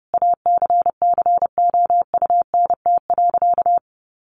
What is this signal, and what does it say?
Morse code, heard as a keyed tone, about 30 wpm, 700 Hz. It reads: ACCOUNT.
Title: Sentences from Top 1400 Words 30wpm